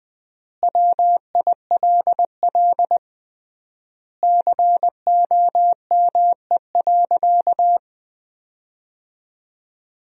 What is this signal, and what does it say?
Morse code, heard as a keyed tone, about 20 wpm, 700 Hz. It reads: WILL COME.
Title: 3x Word-Spacing - Sentences from Top 100 Words - 20wpm